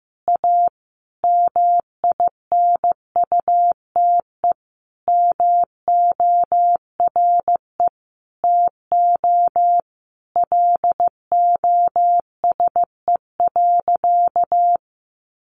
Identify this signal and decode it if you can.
Morse code, heard as a keyed tone, about 15 wpm, 700 Hz. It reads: A MINUTE MORE TO LOSE.
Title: Sentences from Top 1100 Words 15wpm